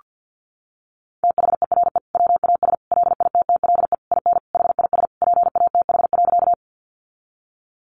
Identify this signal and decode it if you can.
Morse code, heard as a keyed tone, about 48 wpm, 700 Hz. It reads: THERE WAS LITTLE IN HIS PATH.
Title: Sentences from Top 1400 Words 50wpm